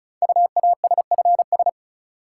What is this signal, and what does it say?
Morse code, heard as a keyed tone, about 36 wpm, 700 Hz. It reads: UASFS
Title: Code Groups - 5 Letters - 35wpm